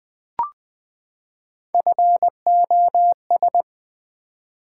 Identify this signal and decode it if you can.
Morse code, heard as a keyed tone, about 20 wpm, 700 Hz. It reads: FOS
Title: Build A Word - 3 Letters At A Time - 6-Letter Words - 20wpm